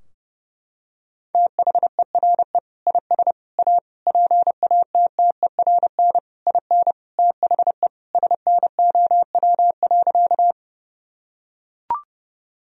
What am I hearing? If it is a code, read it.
Morse code, heard as a keyed tone, about 32 wpm, 700 Hz. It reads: THERE IS A PATTERN IN THE SNOW.
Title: Sentences from Top 1000 Words - Review Part 2 - 30wpm